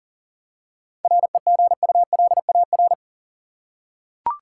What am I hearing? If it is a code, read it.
Morse code, heard as a keyed tone, about 39 wpm, 700 Hz. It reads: REGULAR
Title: Seven Letter Words - 40wpm